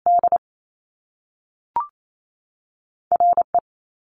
Morse code, heard as a keyed tone, about 28 wpm, 700 Hz. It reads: D RE
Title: Top Two Letter Combinations in Words 28wpm